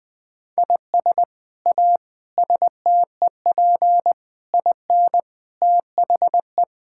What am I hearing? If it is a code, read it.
Morse code, heard as a keyed tone, about 21 wpm, 700 Hz. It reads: IS A STEP IN THE
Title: Sets of 8 Words - 20wpm